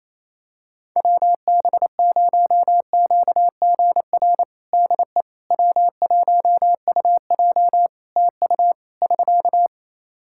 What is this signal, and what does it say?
Morse code, heard as a keyed tone, about 28 wpm, 700 Hz. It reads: WB0QGR DE W1UJ TU <SK>
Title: Practice QSO - Exchange 4 - Ending - 28wpm